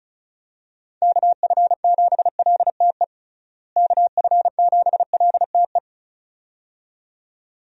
Morse code, heard as a keyed tone, about 36 wpm, 700 Hz. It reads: KF7LTE KF7LTE
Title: USA Call Signs - 6 Characters - Repeated 2x - 35wpm